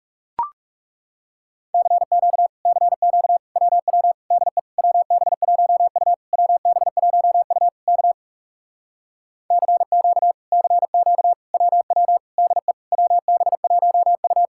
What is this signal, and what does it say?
Morse code, heard as a keyed tone, about 37 wpm, 700 Hz. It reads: CQ CQ WW DE WB1U WB1U K CQ CQ WW DE WB1U
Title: Speed-Racing Common QSO Phrases - 30wpm